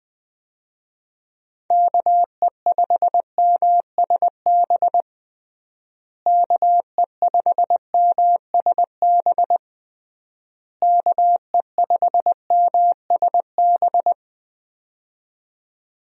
Morse code, heard as a keyed tone, about 20 wpm, 700 Hz. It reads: KE5MSB KE5MSB KE5MSB
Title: USA Call Signs - 6 Characters - Repeated 3x - 20wpm